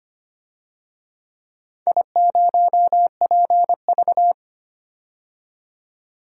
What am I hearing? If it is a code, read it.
Morse code, heard as a keyed tone, about 25 wpm, 700 Hz. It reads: I0PV
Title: DXCC-100 Call Signs - 4 Characters - 25wpm